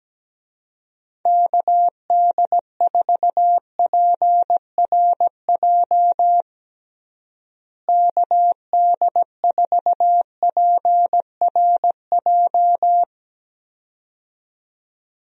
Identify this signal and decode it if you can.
Morse code, heard as a keyed tone, about 17 wpm, 700 Hz. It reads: KD4PRJ KD4PRJ
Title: USA Call Signs - 6 Characters - Repeated 2x - 17wpm